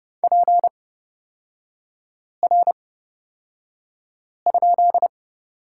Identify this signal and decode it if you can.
Morse code, heard as a keyed tone, about 30 wpm, spaced as for 5 wpm, 700 Hz. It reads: PR?
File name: Lesson 36e - BT - Three Characters 30wpm